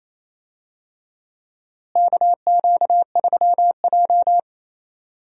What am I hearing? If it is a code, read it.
Morse code, heard as a keyed tone, about 27 wpm, 700 Hz. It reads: KQ3J
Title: USA Call Signs - 4 Characters - Repeated 3x - 28wpm